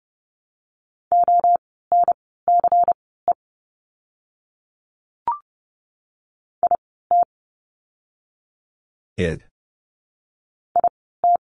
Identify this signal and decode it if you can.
Morse code, heard as a keyed tone, about 30 wpm, spaced as for 17 wpm, 700 Hz. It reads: ONCE IT IT
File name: Lesson 16f - Letter c - Words 30wpm